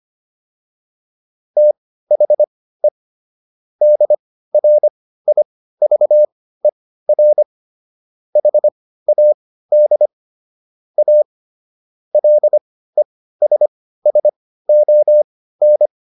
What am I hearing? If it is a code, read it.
Morse code, heard as a keyed tone, about 25 wpm, spaced as for 15 wpm, 600 Hz. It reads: THE DRIVER HAD A LESSON